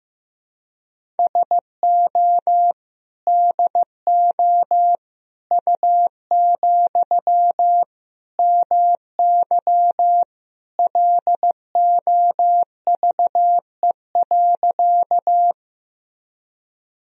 Morse code, heard as a keyed tone, about 15 wpm, 700 Hz. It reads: SO DO U, MY LOVE.
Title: Sentences from Top 400 Words and Abbreviations 15wpm